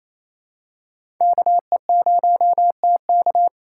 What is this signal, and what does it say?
Morse code, heard as a keyed tone, about 27 wpm, 700 Hz. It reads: KE0TK
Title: USA Call Signs - 4x Character Spacing - 28wpm